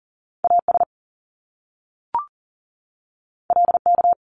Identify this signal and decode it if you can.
Morse code, heard as a keyed tone, about 40 wpm, 700 Hz. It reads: AS LK
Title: Top Two Letter Combinations in Words 40wpm